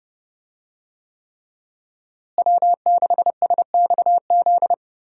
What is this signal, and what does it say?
Morse code, heard as a keyed tone, about 32 wpm, 700 Hz. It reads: W6SXZ
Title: USA Call Signs - 5 Characters - Repeated 3x - 30wpm